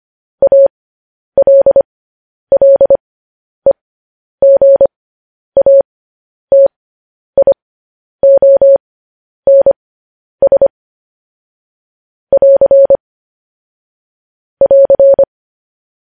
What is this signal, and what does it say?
Morse code, heard as a keyed tone, about 25 wpm, spaced as for 10 wpm, 550 Hz. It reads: ALLEGATIONS <AR> <AR>
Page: news headlines in morse code at 10 wpm